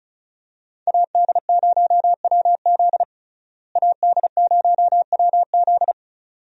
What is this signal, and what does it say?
Morse code, heard as a keyed tone, about 34 wpm, 700 Hz. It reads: AD0WZ AD0WZ
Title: USA Call Signs - 5 Characters - Repeated 2x - 35wpm